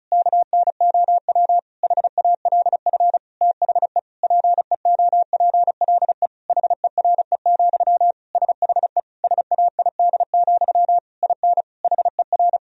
Morse code, heard as a keyed tone, about 34 wpm, 700 Hz. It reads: KNOW HALF THE PEOPLE HERE, SHE SAID, IN HER
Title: Sentences from Top 1000 Words - Review Part 1 - 35wpm